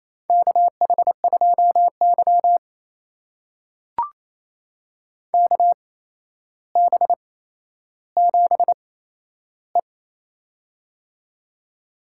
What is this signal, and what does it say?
Morse code, heard as a keyed tone, about 28 wpm, spaced as for 12 wpm, 700 Hz. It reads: KH2Y KB7E
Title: USA Call Signs - 4 Characters - 8x Character Spacing - 28wpm